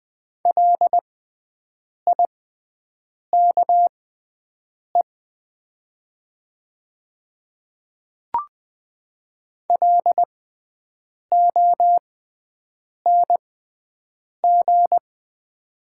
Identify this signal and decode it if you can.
Morse code, heard as a keyed tone, about 20 wpm, spaced as for 7 wpm, 700 Hz. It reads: LIKE LONG
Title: 6x Character Spacing - Top 200 Words - 20wpm